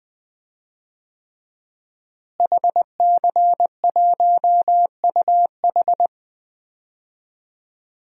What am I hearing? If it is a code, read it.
Morse code, heard as a keyed tone, about 20 wpm, 700 Hz. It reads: HC1UH
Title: DXCC-100 Call Signs - 5 Characters - Repeated 3x - 20wpm